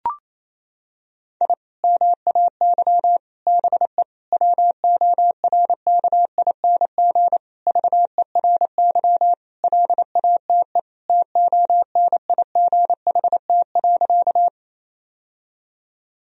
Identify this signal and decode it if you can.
Morse code, heard as a keyed tone, about 28 wpm, 700 Hz. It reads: I MAY BE WORKING VERY LATE TONIGHT.
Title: Sentences from Top 1000 Words - Review Part 4 - 28wpm